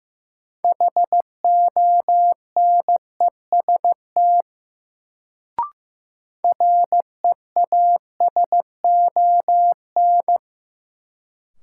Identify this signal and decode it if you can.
Morse code, heard as a keyed tone, about 15 wpm, 700 Hz. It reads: HONEST REASON
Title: Sets of 5 Words Encore - New Word Drill - 15wpm